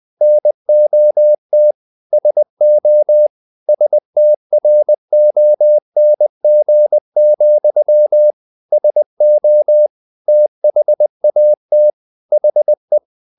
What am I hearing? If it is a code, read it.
Morse code, heard as a keyed tone, about 20 wpm, 600 Hz. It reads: NOT SO STRONG, SO THAT HE
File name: the old man and the sea / the old man and the sea part 1 20wpm